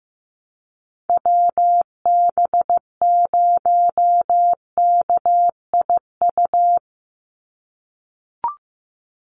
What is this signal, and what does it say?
Morse code, heard as a keyed tone, about 15 wpm, 700 Hz. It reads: WB0KIU
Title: NA SOTA Chaser Callsigns 15wpm